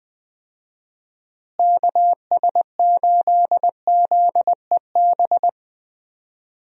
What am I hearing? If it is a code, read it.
Morse code, heard as a keyed tone, about 20 wpm, 700 Hz. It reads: KS8ZEB
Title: USA Call Signs - 6 Characters - Repeated 3x - 20wpm